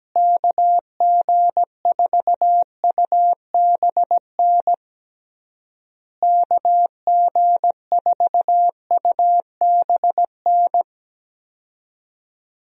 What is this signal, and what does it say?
Morse code, heard as a keyed tone, about 17 wpm, 700 Hz. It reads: KG4UBN KG4UBN
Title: USA Call Signs - 6 Characters - Repeated 3x - 17wpm